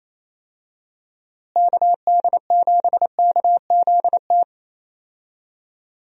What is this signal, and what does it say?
Morse code, heard as a keyed tone, about 28 wpm, 700 Hz. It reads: KD7KZT
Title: USA Call Signs - 6 Characters - Repeated 2x - 28wpm